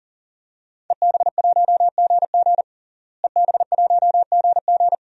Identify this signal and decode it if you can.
Morse code, heard as a keyed tone, about 40 wpm, 700 Hz. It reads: EB1GG EB1GG
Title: DXCC-100 Call Signs - 5 Characters - Repeated 2x - 40wpm